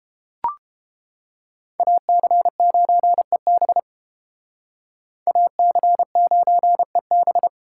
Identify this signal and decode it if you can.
Morse code, heard as a keyed tone, about 30 wpm, 700 Hz. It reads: AC9EB AC9EB